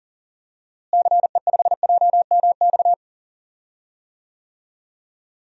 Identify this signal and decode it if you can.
Morse code, heard as a keyed tone, about 40 wpm, 700 Hz. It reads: CE5JMX
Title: DXCC-100 Call Signs - 6 Characters - Repeated 2x - 40wpm